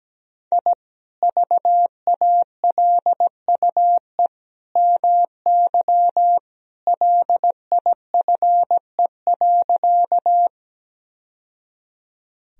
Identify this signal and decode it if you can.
Morse code, heard as a keyed tone, about 17 wpm, 700 Hz. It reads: I VALUE MY LIFE.
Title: Sentences from Top 1000 Words - Review Part 3 - 17wpm